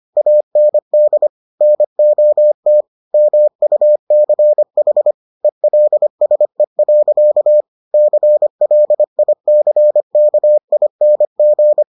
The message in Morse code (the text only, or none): AND NOT MUCH ELSE. CLICKING